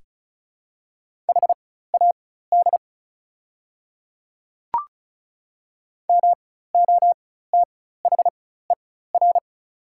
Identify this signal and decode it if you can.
Morse code, heard as a keyed tone, about 34 wpm, spaced as for 16 wpm, 690 Hz. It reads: HAD MOTHER
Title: 4x Character Spacing - Top 300 Words - 35wpm